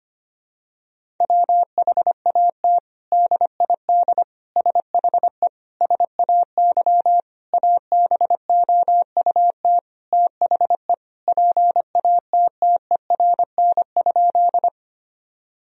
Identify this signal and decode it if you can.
Morse code, heard as a keyed tone, about 25 wpm, 700 Hz. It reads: WHAT DID SHE SAY ABOUT THE PATTERN?
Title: Sentences from Top 1000 Words - Review Part 2 - 25wpm